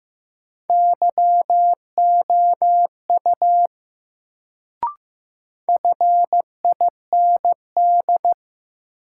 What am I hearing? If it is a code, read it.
Morse code, heard as a keyed tone, about 15 wpm, 700 Hz. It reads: YOU FIND
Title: Sentences from Top 100 Words - New Word Drill - 15wpm